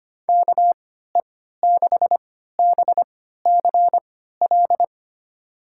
Morse code, heard as a keyed tone, about 25 wpm, spaced as for 14 wpm, 700 Hz. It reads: KE6BCL
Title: USA Call Signs - 6 Characters - 3x Character Spacing - 25wpm